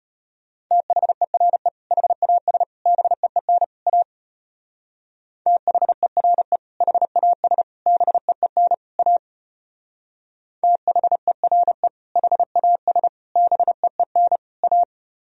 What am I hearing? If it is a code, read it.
Morse code, heard as a keyed tone, about 32 wpm, 700 Hz. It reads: THERE HAS BEEN A THERE HAS BEEN A THERE HAS BEEN A